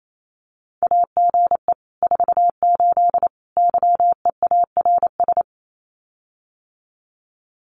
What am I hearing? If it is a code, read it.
Morse code, heard as a keyed tone, about 28 wpm, 700 Hz. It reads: AGE 48 YEARS
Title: Common QSO Phrases Encore 28wpm